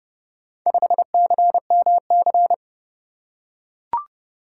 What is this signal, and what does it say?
Morse code, heard as a keyed tone, about 27 wpm, 700 Hz. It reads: 5CMC